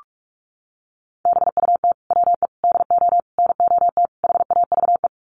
Morse code, heard as a keyed tone, about 44 wpm, 700 Hz. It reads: BUT WE DO NOT HAVE
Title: Sets of 5 Words Encore2 45wpm